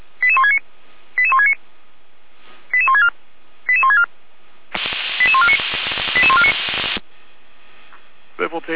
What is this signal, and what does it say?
Five-tone selective call (ZVEI), heard as quick runs of five tones, tones of about 70 ms, with speech. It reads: 88169 88169 88164 88164 88149 88149